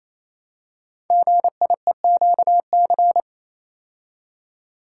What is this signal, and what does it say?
Morse code, heard as a keyed tone, about 27 wpm, 700 Hz. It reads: GIEQC
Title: Code Groups - 5 Letter:Numbers - 28wpm